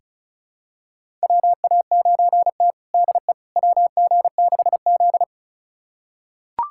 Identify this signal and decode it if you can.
Morse code, heard as a keyed tone, about 36 wpm, 700 Hz. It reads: WA9T DE WG6Z
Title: USA Callsign DE Callsign - 4 Characters - 35wpm